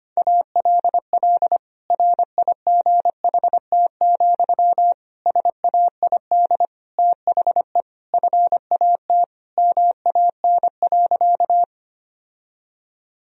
Morse code, heard as a keyed tone, about 25 wpm, 700 Hz. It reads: ALL RIGHT, SAID THE FAT MAN.
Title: Sentences from Top 1000 Words - Review Part 4 - 25wpm